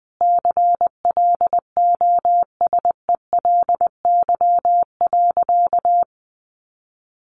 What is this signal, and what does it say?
Morse code, heard as a keyed tone, about 20 wpm, 700 Hz. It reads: CLOSELY.
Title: Sentences from Top 1400 Words 20wpm